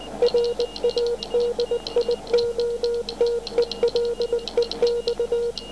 Morse code, heard as a keyed tone, about 19 wpm, 480 Hz. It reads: RADIOTELEX